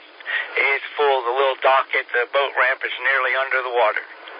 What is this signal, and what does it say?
Emergency Alert System (SAME) tones, heard as harsh buzzing bursts, with speech. It is heard from 0.5 to 3.5 s.